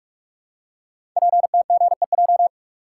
Morse code, heard as a keyed tone, about 44 wpm, 700 Hz. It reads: PTGEJ
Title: Code Groups - 5 Letters - 45wpm